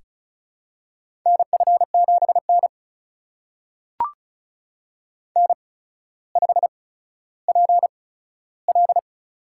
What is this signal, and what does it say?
Morse code, heard as a keyed tone, about 34 wpm, spaced as for 15 wpm, 700 Hz. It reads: NF7N N5PL
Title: USA Call Signs - 4 Characters - 8x Character Spacing - 35wpm